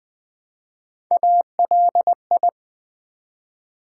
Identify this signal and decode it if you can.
Morse code, heard as a keyed tone, about 20 wpm, 700 Hz. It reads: ALI
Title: Build A Word - 1 Letter At A Time - 3 to 5-Letter Words - 20wpm